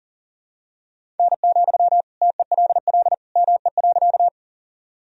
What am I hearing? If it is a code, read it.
Morse code, heard as a keyed tone, about 40 wpm, 700 Hz. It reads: N, TELL ME.
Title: Sentences from Top 200 Words and Abbreviations 40wpm